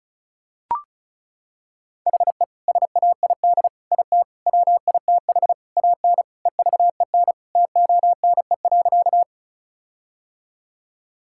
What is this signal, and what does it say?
Morse code, heard as a keyed tone, about 34 wpm, 700 Hz. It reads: HE SAID IT WITH AN EVEN TONE.
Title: Sentences from Top 1600 Words 35wpm